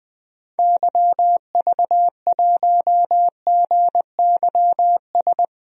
Morse code, heard as a keyed tone, about 20 wpm, 700 Hz. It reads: YV1GYS
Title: DXCC-100 Call Signs - 6 Characters - 20wpm